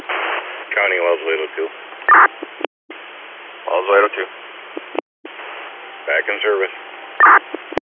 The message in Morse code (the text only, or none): none